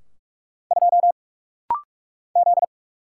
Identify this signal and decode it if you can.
Morse code, heard as a keyed tone, about 44 wpm, 700 Hz. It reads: 2 Z
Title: Single Letter-Number Alt - Rapid-Fire 45wpm